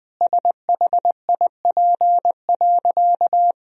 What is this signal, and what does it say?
Morse code, heard as a keyed tone, about 20 wpm, 700 Hz. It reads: SHIP.
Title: Sentences from Top 1000 Words - Review Part 2 - 20wpm